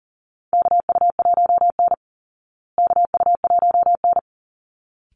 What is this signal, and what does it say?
Morse code, heard as a keyed tone, about 40 wpm, 700 Hz. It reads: KU1N KU1N
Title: Call Signs Repeated 3x - 40wpm